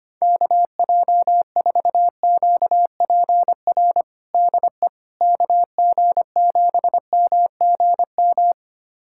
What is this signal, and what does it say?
Morse code, heard as a keyed tone, about 25 wpm, 700 Hz. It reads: KJ4QPR DE KG7MGM
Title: USA Callsign DE Callsign 25wpm